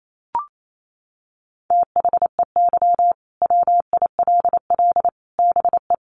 Morse code, heard as a keyed tone, about 28 wpm, 700 Hz. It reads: THEY WILL BE
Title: Sets of 5 Words Encore 28wpm